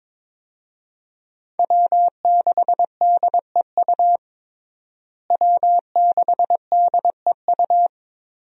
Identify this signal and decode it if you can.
Morse code, heard as a keyed tone, about 22 wpm, 700 Hz. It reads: W6DEU W6DEU
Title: USA Call Signs - 5 Characters - Repeated 3x - 22wpm